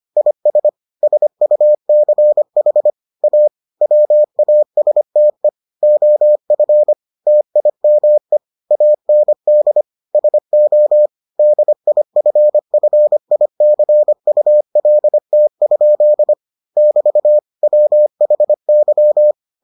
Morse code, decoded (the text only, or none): IS SUCH A WASTE OF TIME AND SO DIFFICULT? <BT> WHY